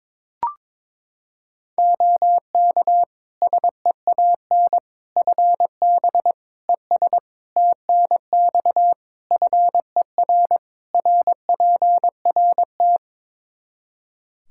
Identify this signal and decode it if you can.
Morse code, heard as a keyed tone, about 22 wpm, 700 Hz. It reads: OK SEAN FB ES TNX FER RPRT
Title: Practice QSO - Exchange 2 - Info - 22wpm